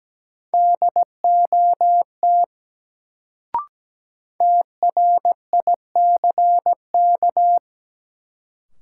Morse code, heard as a keyed tone, about 17 wpm, 700 Hz. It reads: DOT TRICK
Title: Sentences from Top 1500 Words - New Word Drill Alt - 17wpm